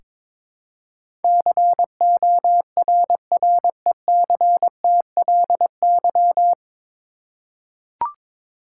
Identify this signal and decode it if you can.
Morse code, heard as a keyed tone, about 22 wpm, 700 Hz. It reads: CORRECTLY
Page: Morse Code Ninja practice 22wpm